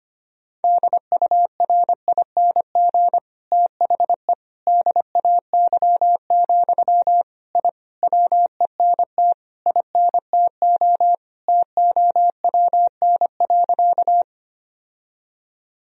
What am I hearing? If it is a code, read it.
Morse code, heard as a keyed tone, about 25 wpm, 700 Hz. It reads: DURING THE DAY, I WENT INTO TOWN.
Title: Sentences from Top 1000 Words - Review Part 2 - 25wpm